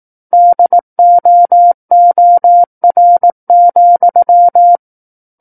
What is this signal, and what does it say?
Morse code, heard as a keyed tone, about 18 wpm, 700 Hz. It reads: DOOR,
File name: Book One / 09-Chapter 9 - 18 WPM